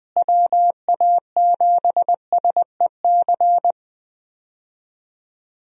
Morse code, heard as a keyed tone, about 20 wpm, 700 Hz. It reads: WA7SEC
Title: USA Call Signs - 6 Characters - Repeated 3x - 20wpm